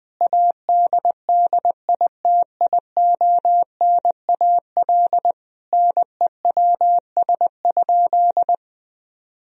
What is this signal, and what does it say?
Morse code, heard as a keyed tone, about 20 wpm, 700 Hz. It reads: ADDITIONAL NEWS?
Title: Sentences from Top 1000 Words - Review Part 4 - 20wpm